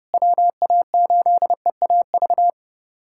Morse code, heard as a keyed tone, about 30 wpm, 700 Hz. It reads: WA8EAV
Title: USA Callsign DE Callsign 30wpm